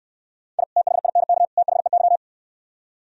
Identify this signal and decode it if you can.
Morse code, heard as a keyed tone, about 66 wpm, 700 Hz. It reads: I THINK THEY